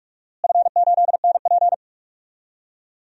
Morse code, heard as a keyed tone, about 46 wpm, 690 Hz. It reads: F8NP
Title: DXCC-100 Call Signs - 4 Characters - Repeated 3x - 45wpm